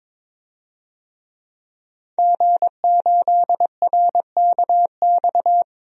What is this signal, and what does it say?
Morse code, heard as a keyed tone, about 22 wpm, 700 Hz. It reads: G8RKX
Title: DXCC-100 Call Signs - 5 Characters - 22wpm